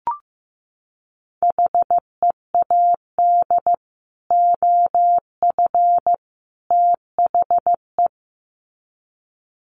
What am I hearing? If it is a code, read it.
Morse code, heard as a keyed tone, about 15 wpm, 700 Hz. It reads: HEAD OF THE
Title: Sets of 3 Words 15wpm